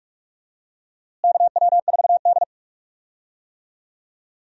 Morse code, heard as a keyed tone, about 46 wpm, 700 Hz. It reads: KW4D
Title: USA Call Signs - 4 Characters - Repeated 3x - 45wpm